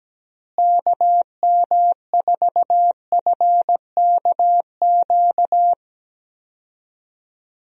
Morse code, heard as a keyed tone, about 17 wpm, 700 Hz. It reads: KM4FKQ
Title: USA Call Signs - 6 Characters - Repeated 2x - 17wpm